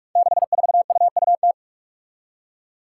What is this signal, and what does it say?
Morse code, heard as a keyed tone, about 44 wpm, 700 Hz. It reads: 64UUT
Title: Five Challenging Letter-Number Combinations 45wpm